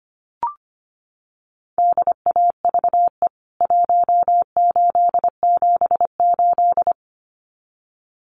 Morse code, heard as a keyed tone, about 25 wpm, 700 Hz. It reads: DAVE 1878
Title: CWT Exchange 25wpm